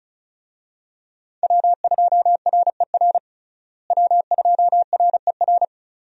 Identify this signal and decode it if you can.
Morse code, heard as a keyed tone, about 36 wpm, 700 Hz. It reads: W2RER W2RER
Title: USA Call Signs - 5 Characters - Repeated 2x - 35wpm